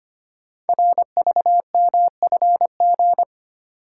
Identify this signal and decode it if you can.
Morse code, heard as a keyed tone, about 25 wpm, 700 Hz. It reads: RVMFG
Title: Code Groups - 5 Letter:Numbers - 25wpm